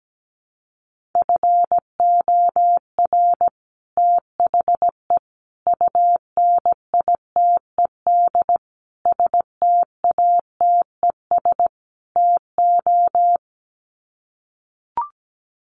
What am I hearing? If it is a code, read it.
Morse code, heard as a keyed tone, about 17 wpm, 700 Hz. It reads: FOR THE UNITED STATES TO
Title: Sets of 5 Words Encore 17wpm